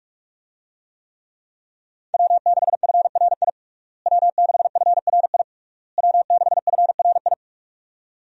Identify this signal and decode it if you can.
Morse code, heard as a keyed tone, about 46 wpm, 700 Hz. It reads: W6FRI W6FRI W6FRI
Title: USA Call Signs - 5 Characters - Repeated 3x - 45wpm